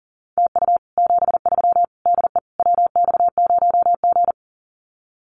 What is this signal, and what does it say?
Morse code, heard as a keyed tone, about 40 wpm, 700 Hz. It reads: TU 73 DE WX0G
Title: Common QSO Phrases Encore 40wpm